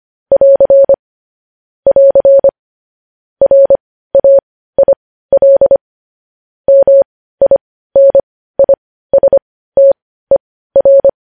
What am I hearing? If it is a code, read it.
Morse code, heard as a keyed tone, about 25 wpm, spaced as for 15 wpm, 550 Hz. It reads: <AR> <AR> RAIL MINISTER
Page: News Headlines in Morse Code at 15 WPM